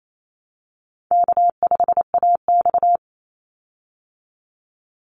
Morse code, heard as a keyed tone, about 28 wpm, 700 Hz. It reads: K5AX
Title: Easy Call Signs - 28wpm